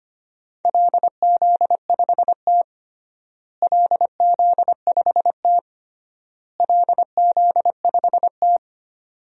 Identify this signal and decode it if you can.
Morse code, heard as a keyed tone, about 25 wpm, 700 Hz. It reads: LZ5T LZ5T LZ5T
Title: DXCC-100 Call Signs - 4 Characters - Repeated 3x - 25wpm